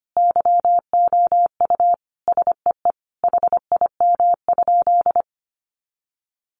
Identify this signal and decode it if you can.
Morse code, heard as a keyed tone, about 25 wpm, 700 Hz. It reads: YOU SEE HIM?
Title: Sentences from Top 100 Words 25wpm